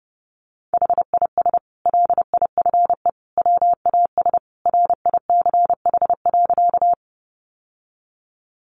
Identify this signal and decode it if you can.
Morse code, heard as a keyed tone, about 30 wpm, 700 Hz. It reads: HIS LIFE WAS RICH.
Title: Sentences from Top 1100 Words 30wpm